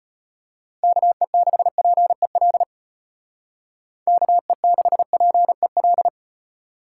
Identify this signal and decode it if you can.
Morse code, heard as a keyed tone, about 30 wpm, 700 Hz. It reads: KE6PEL KE6PEL
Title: Speed Racing - USA 2x3 Call Signs 25wpm